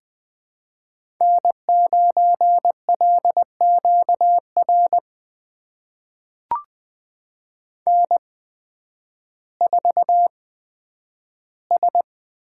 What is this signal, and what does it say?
Morse code, heard as a keyed tone, about 20 wpm, spaced as for 11 wpm, 700 Hz. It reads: N9LQR N4S
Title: USA Call Signs - 8x Character Spacing - 20wpm